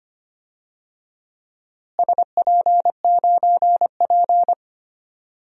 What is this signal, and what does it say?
Morse code, heard as a keyed tone, about 25 wpm, 700 Hz. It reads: SP9P
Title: DXCC-100 Call Signs - 4 Characters - Repeated 2x - 25wpm